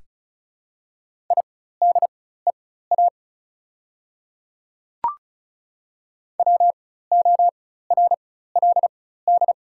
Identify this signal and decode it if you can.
Morse code, heard as a keyed tone, about 36 wpm, spaced as for 16 wpm, 690 Hz. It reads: IDEA WORLD
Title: 4x Character Spacing - Top 500 Words - 35wpm